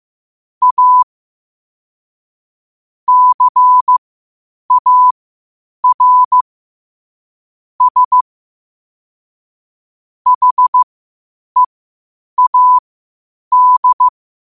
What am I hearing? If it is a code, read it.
Morse code, heard as a keyed tone, about 15 wpm, spaced as for 8 wpm, 1000 Hz. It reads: A CAR S HEAD
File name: qotdfiles / 8wpm